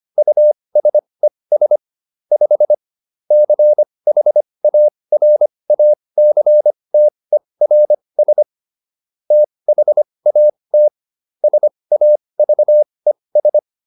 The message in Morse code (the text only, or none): USES 5 CHARACTERS THAT SAVES